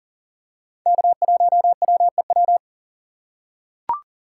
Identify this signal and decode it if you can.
Morse code, heard as a keyed tone, about 40 wpm, 700 Hz. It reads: K1WEW
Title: USA Call Signs - 5 Characters - 3x Character Spacing - 40wpm